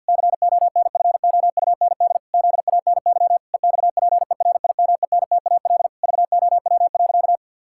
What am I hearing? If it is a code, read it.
Morse code, heard as a keyed tone, about 50 wpm, 700 Hz. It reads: CONFOUND ZANY EXPERIMENTAL VOW.
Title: Pangram Challenge v2 - 50wpm